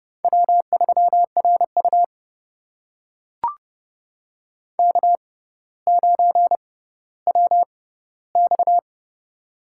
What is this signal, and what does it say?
Morse code, heard as a keyed tone, about 29 wpm, spaced as for 15 wpm, 700 Hz. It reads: W3RU K9WX
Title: USA Call Signs - 4 Characters - 6x Character Spacing - 30wpm